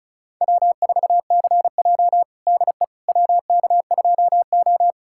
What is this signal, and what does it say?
Morse code, heard as a keyed tone, about 36 wpm, 700 Hz. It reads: W4CJ DE WK2O
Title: USA Callsign DE Callsign - 4 Characters - 35wpm